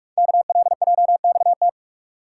Morse code, heard as a keyed tone, about 44 wpm, 700 Hz. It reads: KLJXT
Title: Code Groups - 5 Letter:Numbers - 45wpm